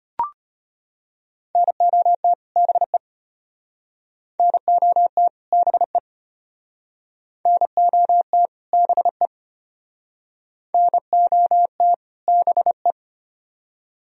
Morse code, heard as a keyed tone, about 33 wpm, 700 Hz. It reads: NOT BE NOT BE NOT BE NOT BE